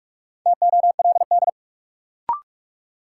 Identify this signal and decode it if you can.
Morse code, heard as a keyed tone, about 42 wpm, 700 Hz. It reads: TOLD
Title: Sentences from Top 400 Words - New Word Drill - 45wpm